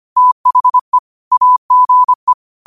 Morse code, heard as a keyed tone, about 25 wpm, 1000 Hz. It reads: THE AGE
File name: 25wpm